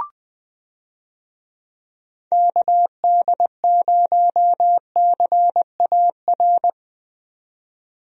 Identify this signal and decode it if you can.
Morse code, heard as a keyed tone, about 20 wpm, 700 Hz. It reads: KD0CAR
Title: USA Call Signs - 6 Characters - 20wpm